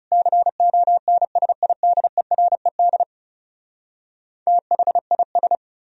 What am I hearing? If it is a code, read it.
Morse code, heard as a keyed tone, about 40 wpm, 700 Hz. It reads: CONSIDERED THIS